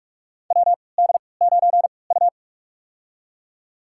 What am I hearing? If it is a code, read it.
Morse code, heard as a keyed tone, about 42 wpm, spaced as for 25 wpm, 700 Hz. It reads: WD9U
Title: USA Call Signs - 3x Character Spacing - 45wpm